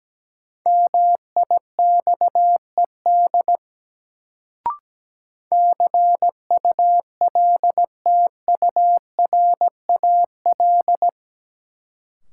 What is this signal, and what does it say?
Morse code, heard as a keyed tone, about 17 wpm, 700 Hz. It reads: MIXED CULTURAL
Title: Sentences from Top 1500 Words - New Word Drill Alt - 17wpm